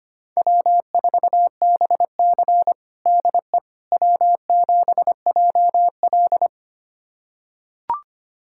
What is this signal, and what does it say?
Morse code, heard as a keyed tone, about 25 wpm, 700 Hz. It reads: W4BC DE W7JL
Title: USA Callsign DE Callsign - 4 Characters - 25wpm